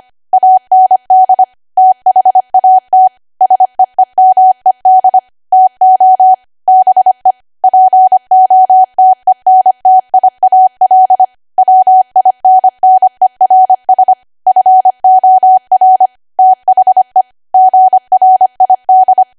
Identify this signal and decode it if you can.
Morse code, heard as a keyed tone, about 25 wpm, 750 Hz. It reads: AND THAT SEEMED TO BE POTENTIAL WINNERS FOR THE GRID
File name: Archive / 25 WPM / 25WPM